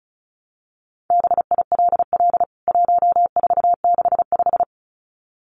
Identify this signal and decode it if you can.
Morse code, heard as a keyed tone, about 34 wpm, 700 Hz. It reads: BILL 1465
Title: CWT Exchange 35wpm